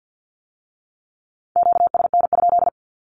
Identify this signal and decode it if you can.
Morse code, heard as a keyed tone, about 50 wpm, 700 Hz. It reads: QSN?
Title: Q-Codes with Spoken Meaning - 50wpm